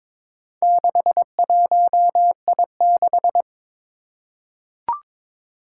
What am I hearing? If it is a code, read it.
Morse code, heard as a keyed tone, about 22 wpm, 700 Hz. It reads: 61I6
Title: Four Challenging Letter-Number Combinations 22wpm